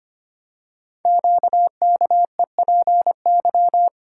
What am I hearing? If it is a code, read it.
Morse code, heard as a keyed tone, about 25 wpm, 700 Hz. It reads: QKEPY
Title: Code Groups - 5 Letters - 25wpm